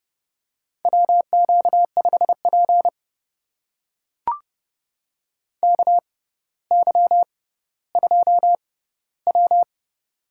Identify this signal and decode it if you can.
Morse code, heard as a keyed tone, about 30 wpm, spaced as for 15 wpm, 700 Hz. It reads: WQ5P KY2W
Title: USA Call Signs - 6x Character Spacing - 30wpm